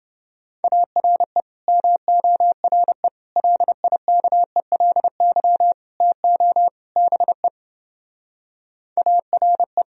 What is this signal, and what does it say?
Morse code, heard as a keyed tone, about 30 wpm, 700 Hz. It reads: ARE MORE LIKELY TO BE ARE